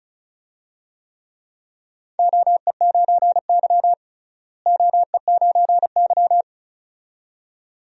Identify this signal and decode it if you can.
Morse code, heard as a keyed tone, about 36 wpm, 700 Hz. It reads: OE9Y OE9Y
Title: DXCC-100 Call Signs - 4 Characters - Repeated 2x - 35wpm